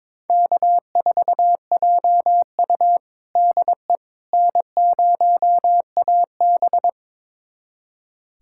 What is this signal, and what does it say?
Morse code, heard as a keyed tone, about 22 wpm, 700 Hz. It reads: K4JU DE N0AB
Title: USA Callsign DE Callsign - 4 Characters - 22wpm